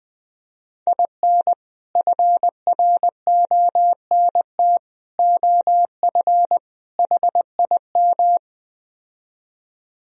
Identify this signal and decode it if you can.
Morse code, heard as a keyed tone, about 20 wpm, 700 Hz. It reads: IN FRONT OF HIM